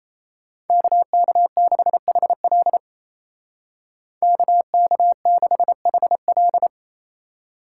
Morse code, heard as a keyed tone, about 33 wpm, 700 Hz. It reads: KK6HL KK6HL
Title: Speed Racing - USA 2x2 Call Signs 28wpm